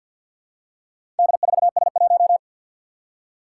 Morse code, heard as a keyed tone, about 51 wpm, 700 Hz. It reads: D4S1
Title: Four Challenging Letter-Number Combinations 50wpm